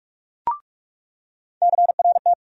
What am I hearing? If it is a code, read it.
Morse code, heard as a keyed tone, about 46 wpm, 700 Hz. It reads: CRT
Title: USA Call Sign Postfixes - 3 Letters 45wpm